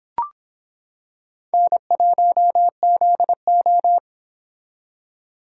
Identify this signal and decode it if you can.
Morse code, heard as a keyed tone, about 25 wpm, 700 Hz. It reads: N1ZO